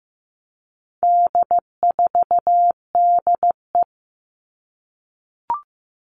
Morse code, heard as a keyed tone, about 15 wpm, 700 Hz. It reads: D4DE
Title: All Cut Numbers - 4-Digit 15wpm